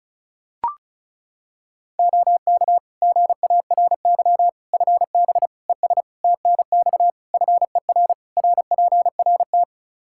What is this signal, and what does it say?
Morse code, heard as a keyed tone, about 35 wpm, 700 Hz. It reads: OK GARY FB ES TNX FER RPRT
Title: Practice QSO - Exchange 2 - Info - 35wpm